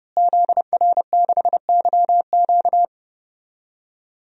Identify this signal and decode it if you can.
Morse code, heard as a keyed tone, about 31 wpm, 700 Hz. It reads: ZR6YQ
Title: DXCC-100 Call Signs - 5 Characters - Repeated 3x - 30wpm